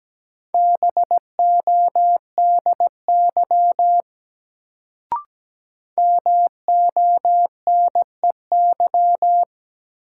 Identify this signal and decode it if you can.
Morse code, heard as a keyed tone, about 17 wpm, 700 Hz. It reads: BODY MONEY
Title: Sentences from Top 400 Words - New Word Drill - 17wpm